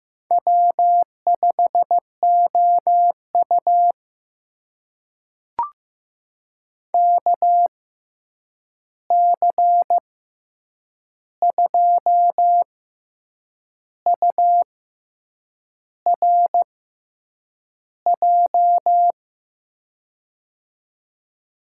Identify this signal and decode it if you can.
Morse code, heard as a keyed tone, about 15 wpm, spaced as for 7 wpm, 700 Hz. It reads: W5OU KC2URJ
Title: USA Call Signs - 6x Character Spacing - 15wpm